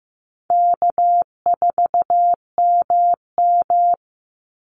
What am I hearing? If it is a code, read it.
Morse code, heard as a keyed tone, about 15 wpm, 700 Hz. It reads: K4MM
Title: Easy Call Signs Repeated 3x - 15wpm